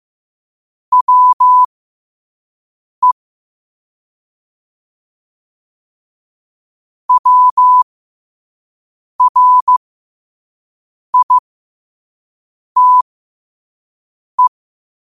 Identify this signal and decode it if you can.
Morse code, heard as a keyed tone, about 15 wpm, spaced as for 5 wpm, 1000 Hz. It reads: WE WRITE
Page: 5 WPM morse code quotes